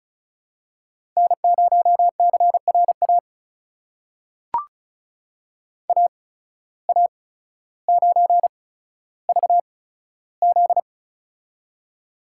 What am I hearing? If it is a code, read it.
Morse code, heard as a keyed tone, about 36 wpm, spaced as for 15 wpm, 700 Hz. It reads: N0CRA AA9VZ
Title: USA Call Signs - 5 Characters - 8x Character Spacing - 35wpm